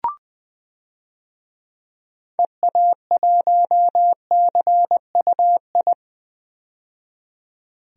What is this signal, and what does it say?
Morse code, heard as a keyed tone, about 20 wpm, 700 Hz. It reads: EA1CUI
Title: DXCC-100 Call Signs - 6 Characters - 20wpm